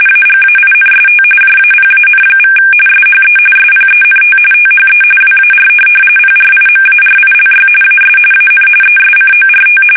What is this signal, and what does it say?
RTTY, about 45.45 baud. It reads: AGAINST JAPAN= SHE SPENT THE WAR PRIMARILY AS AN ESCORT
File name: fsk850 / USS-Massachusetts